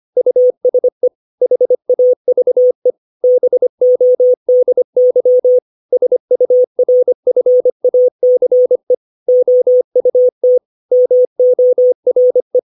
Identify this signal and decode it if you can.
Morse code, heard as a keyed tone, about 25 wpm, 500 Hz. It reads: USE HAVE BODY SURFACE OUT MORE
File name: 632 words / 632 Words 25WPM